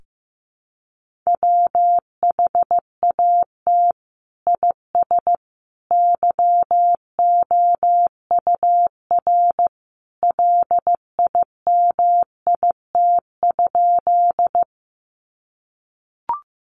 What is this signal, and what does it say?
Morse code, heard as a keyed tone, about 15 wpm, 700 Hz. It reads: WHAT IS YOUR LIMIT?
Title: Sentences from Top 1500 Words 15wpm